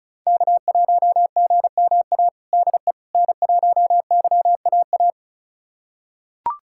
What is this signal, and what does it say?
Morse code, heard as a keyed tone, about 36 wpm, 700 Hz. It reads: K1GMA DE N1YAA
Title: USA Callsign DE Callsign - 5 Characters - 35wpm